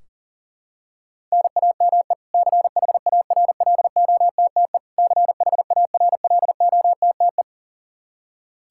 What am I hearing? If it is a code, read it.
Morse code, heard as a keyed tone, about 40 wpm, 700 Hz. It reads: NAME CHARLOTTE CHARLOTTE
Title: Practice QSO - Exchange 1 - Beginning - 40wpm